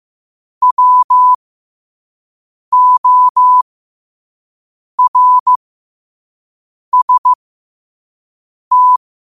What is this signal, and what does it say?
Morse code, heard as a keyed tone, about 15 wpm, spaced as for 5 wpm, 1000 Hz. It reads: WORST
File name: qotdfiles / 5wpm